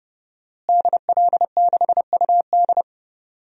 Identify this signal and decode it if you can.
Morse code, heard as a keyed tone, about 31 wpm, 700 Hz. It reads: DL6UD
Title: DXCC-100 Call Signs - 5 Characters - Repeated 3x - 30wpm